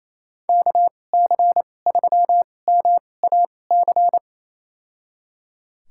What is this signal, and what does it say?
Morse code, heard as a keyed tone, about 28 wpm, spaced as for 20 wpm, 700 Hz. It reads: KC3MAC
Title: USA Call Signs - 6 Characters - 2x Character Spacing - 28wpm